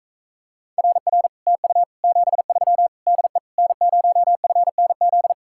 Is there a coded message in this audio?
Yes: Morse code from 0.8 to 5.3 s, heard as a keyed tone, about 45 wpm, 700 Hz. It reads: RR TU 73 DE N0FNZ